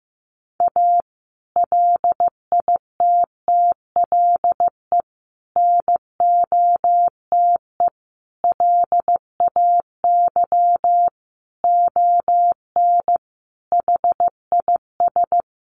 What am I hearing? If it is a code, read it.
Morse code, heard as a keyed tone, about 15 wpm, 700 Hz. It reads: A LITTLE NOTE LAY ON HIS
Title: Sentences from Top 1500 Words 15wpm